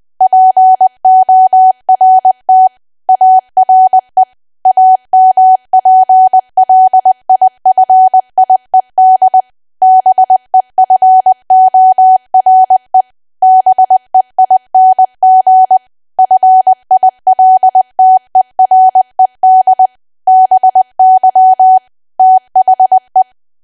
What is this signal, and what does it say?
Morse code, heard as a keyed tone, about 20 wpm, 750 Hz. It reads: PORT ARE AMPLIFIED BEFORE BEING FILTERED BY THE
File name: cwa / W1AW-Level3 / 20WPM